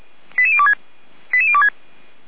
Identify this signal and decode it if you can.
Five-tone selective call (ZVEI), heard as quick runs of five tones, tones of about 70 ms. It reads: 88026 88026